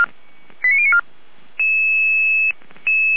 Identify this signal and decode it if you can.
Five-tone selective call (ZVEI), heard as quick runs of five tones, tones of about 70 ms. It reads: 89094